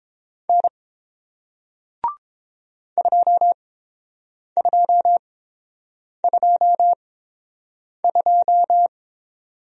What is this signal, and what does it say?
Morse code, heard as a keyed tone, about 30 wpm, spaced as for 7 wpm, 700 Hz. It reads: N 2222